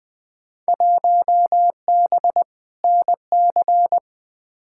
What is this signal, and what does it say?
Morse code, heard as a keyed tone, about 20 wpm, 700 Hz. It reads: 1B NC
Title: Field Day Exchanges - 20wpm